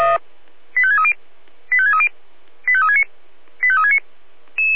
Five-tone selective call (ZVEI), heard as quick runs of five tones, tones of about 70 ms. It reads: 85630 85630 85369 85369